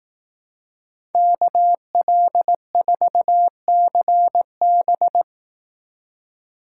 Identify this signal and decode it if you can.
Morse code, heard as a keyed tone, about 18 wpm, 700 Hz. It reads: KL4CB
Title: Speed Racing - USA Call Signs 15wpm